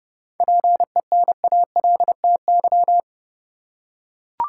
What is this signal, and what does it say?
Morse code, heard as a keyed tone, about 30 wpm, 700 Hz. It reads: PENALTY
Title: Seven Letter Words - 30wpm